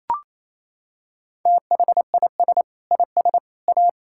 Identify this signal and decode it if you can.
Morse code, heard as a keyed tone, about 28 wpm, 700 Hz. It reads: THIS IS A
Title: Sentences from Top 1600 Words 28wpm